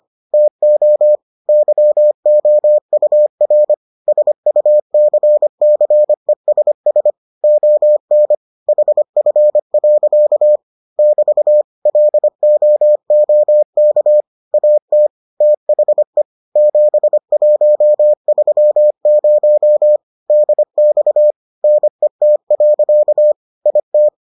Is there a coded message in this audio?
Yes: Morse code from 0.3 to 24.1 s, heard as a keyed tone, about 25 wpm, 600 Hz. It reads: TO YOUR SUCCESS ON HF. <BT> LOOK AT THE 7130 DX NET. IT